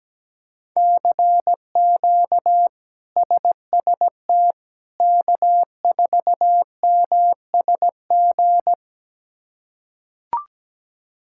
Morse code, heard as a keyed tone, about 17 wpm, 700 Hz. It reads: CQ SST K4MSG
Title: K1USN SST QSO Practice - 17wpm